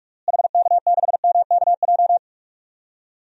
Morse code, heard as a keyed tone, about 42 wpm, 700 Hz. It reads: HK6MKJ
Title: DXCC-100 Call Signs - 6 Characters - Repeated 3x - 45wpm